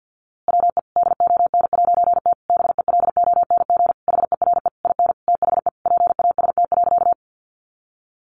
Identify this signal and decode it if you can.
Morse code, heard as a keyed tone, about 50 wpm, 700 Hz. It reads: WE DON'T BELONG HERE IN THE PAST.
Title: A Sound Of Thunder 50wpm